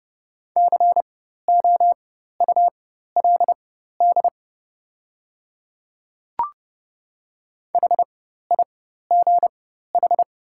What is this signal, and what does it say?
Morse code, heard as a keyed tone, about 30 wpm, spaced as for 14 wpm, 700 Hz. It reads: COULD HIGH
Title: 4x Character Spacing - Top 300 Words - 30wpm